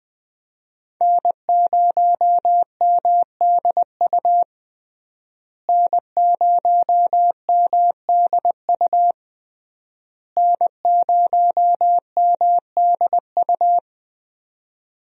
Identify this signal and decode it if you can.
Morse code, heard as a keyed tone, about 20 wpm, 700 Hz. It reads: N0MDU N0MDU N0MDU
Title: USA Call Signs - 5 Characters - Repeated 3x - 20wpm